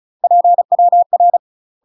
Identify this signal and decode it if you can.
Morse code, heard as a keyed tone, about 35 wpm, 700 Hz. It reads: PWR
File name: UK MASTER QSOs TRAINING/35WPM